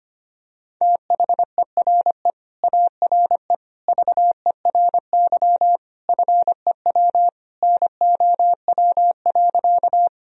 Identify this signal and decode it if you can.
Morse code, heard as a keyed tone, about 25 wpm, 700 Hz. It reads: THERE ARE VERY FEW NOW.
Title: Sentences from Top 1000 Words - Review Part 1 - 25wpm